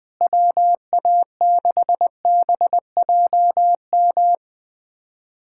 Morse code, heard as a keyed tone, about 20 wpm, 700 Hz. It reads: WA6BJM
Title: USA Callsign DE Callsign 20wpm